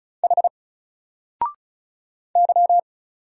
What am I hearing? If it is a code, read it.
Morse code, heard as a keyed tone, about 34 wpm, 700 Hz. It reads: H Y
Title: Single Letter-Number Alt 35wpm